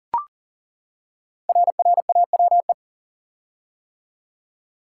Morse code, heard as a keyed tone, about 40 wpm, 700 Hz. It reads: RRAWE
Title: Code Groups - 5 Letters - 40wpm